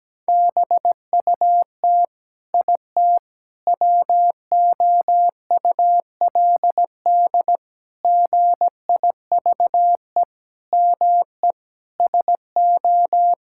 Morse code, heard as a keyed tone, about 17 wpm, 700 Hz. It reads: BUT IT WOULD GIVE ME SO
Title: Sentences from Top 1600 Words 17wpm